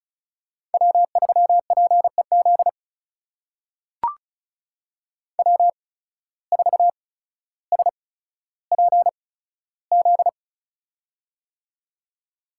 Morse code, heard as a keyed tone, about 36 wpm, spaced as for 15 wpm, 700 Hz. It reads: W3PEZ W4SPZ
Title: USA Call Signs - 5 Characters - 8x Character Spacing - 35wpm